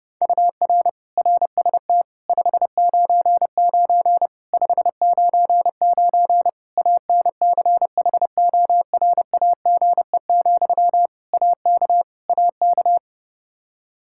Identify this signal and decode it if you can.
Morse code, heard as a keyed tone, about 30 wpm, 700 Hz. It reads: UR RST 599 599 ANCHORAGE, AK AK
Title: Practice QSO - Exchange 1 - Beginning - 30wpm